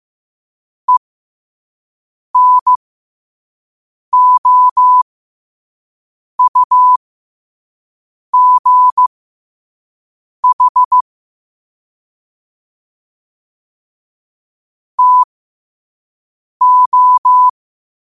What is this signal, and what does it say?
Morse code, heard as a keyed tone, about 15 wpm, spaced as for 5 wpm, 1000 Hz. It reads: ENOUGH TO
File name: qotdfiles / 5wpm